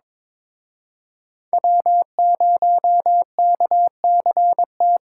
Morse code, heard as a keyed tone, about 26 wpm, 700 Hz. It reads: W0KCT
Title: Speed Racing - USA 1x3 Call Signs 22wpm